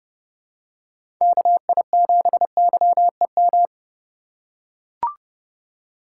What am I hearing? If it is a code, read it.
Morse code, heard as a keyed tone, about 30 wpm, 700 Hz. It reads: KI7YEM
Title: USA Call Signs - 6 Characters - 3x Character Spacing - 30wpm